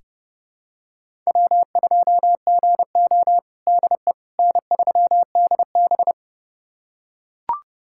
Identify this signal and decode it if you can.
Morse code, heard as a keyed tone, about 30 wpm, 700 Hz. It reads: W2GO DE N3DB
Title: USA Callsign DE Callsign - 4 Characters - 30wpm